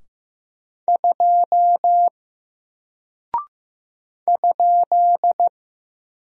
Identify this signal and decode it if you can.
Morse code, heard as a keyed tone, about 15 wpm, 700 Hz. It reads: 2 ?
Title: Single Character (Letter:Number:Punctuation:Prosign) Alt 15wpm